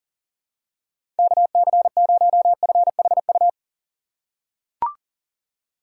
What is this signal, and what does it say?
Morse code, heard as a keyed tone, about 40 wpm, 700 Hz. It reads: KC0FHU
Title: USA Call Signs - 6 Characters - 2x Character Spacing - 40wpm